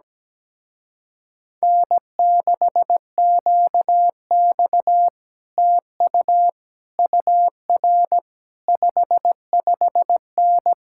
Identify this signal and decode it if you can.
Morse code, heard as a keyed tone, about 17 wpm, 700 Hz. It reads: N6QX TU UR 55N
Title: Speed-Racing Common QSO Phrases - 17wpm